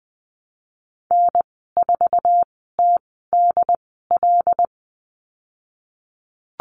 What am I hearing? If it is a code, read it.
Morse code, heard as a keyed tone, about 20 wpm, spaced as for 15 wpm, 700 Hz. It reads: N4TDL
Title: 088 Lesson 15g - Number 5 - Call Signs 20wpm